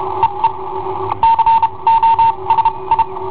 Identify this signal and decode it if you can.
Morse code, heard as a keyed tone, about 32 wpm, spaced as for 25 wpm, 910 Hz. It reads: EE COSI
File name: i7ohp / cassetta i7ohp B